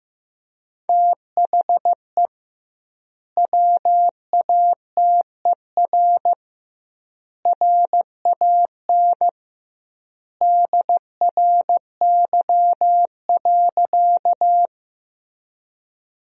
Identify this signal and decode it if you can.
Morse code, heard as a keyed tone, about 15 wpm, 700 Hz. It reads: THE WATER RAN DRY.
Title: 2x Word-Spacing - Sentences from Top 500 Words - 15wpm